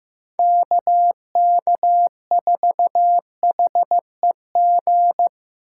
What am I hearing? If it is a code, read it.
Morse code, heard as a keyed tone, about 15 wpm, 700 Hz. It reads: KK4HEG
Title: K1USN SST QSO Practice - 15wpm